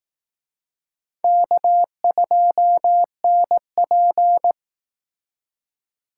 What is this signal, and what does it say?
Morse code, heard as a keyed tone, about 19 wpm, 700 Hz. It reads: K2NP
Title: Speed Racing - USA Call Signs 15wpm